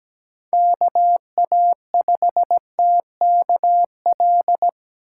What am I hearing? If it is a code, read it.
Morse code, heard as a keyed tone, about 17 wpm, 700 Hz. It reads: KA5TKL
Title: USA Call Signs - 6 Characters - 3x Character Spacing - 17wpm